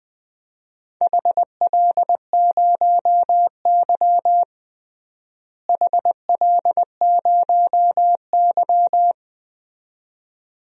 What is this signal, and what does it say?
Morse code, heard as a keyed tone, about 20 wpm, 700 Hz. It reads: HL0Y HL0Y
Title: DXCC-100 Call Signs - 4 Characters - Repeated 2x - 20wpm